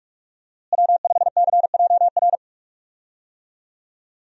Morse code, heard as a keyed tone, about 42 wpm, 700 Hz. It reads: W5CJR
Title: USA Call Signs - 5 Characters - Repeated 3x - 45wpm